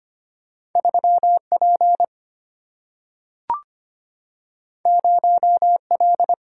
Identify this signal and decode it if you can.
Morse code, heard as a keyed tone, about 25 wpm, 700 Hz. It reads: 3 P 0 L
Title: Single-Letter Single-Number 25wpm